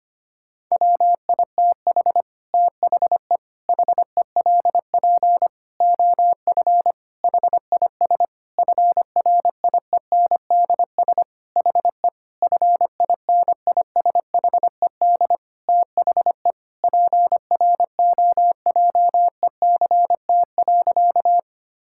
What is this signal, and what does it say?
Morse code, heard as a keyed tone, about 25 wpm, 700 Hz. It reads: WITH THE HELP OF HIS FRIENDS HE FINISHED THE PROJECT.
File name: practice / 25wpm / Sentences with 4-Word Phrases - 25wpm